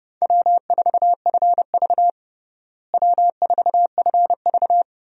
Morse code, heard as a keyed tone, about 30 wpm, 700 Hz. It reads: W4FV W4FV
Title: USA Call Signs - 4 Characters - Repeated 2x - 30wpm